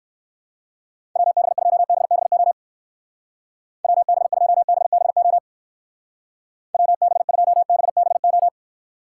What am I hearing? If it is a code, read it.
Morse code, heard as a keyed tone, about 56 wpm, 700 Hz. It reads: WB2BBO WB2BBO WB2BBO